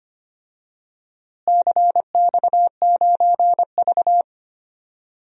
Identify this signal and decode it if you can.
Morse code, heard as a keyed tone, about 25 wpm, 700 Hz. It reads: CX9V
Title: DXCC-100 Call Signs - 4 Characters - Repeated 3x - 25wpm